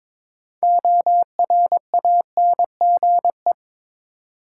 Morse code, heard as a keyed tone, about 22 wpm, 700 Hz. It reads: ORANGE
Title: Six Letter Words - 22wpm